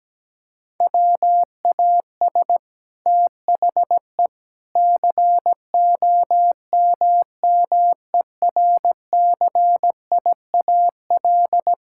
Morse code, heard as a keyed tone, about 17 wpm, 700 Hz. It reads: WAS THE COMMERCIAL
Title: Sentences from Top 1000 Words - Review Part 5 - 17wpm